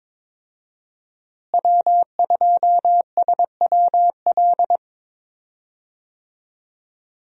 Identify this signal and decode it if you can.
Morse code, heard as a keyed tone, about 22 wpm, 700 Hz. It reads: W2SWL
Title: USA Call Signs - 5 Characters - Repeated 2x - 22wpm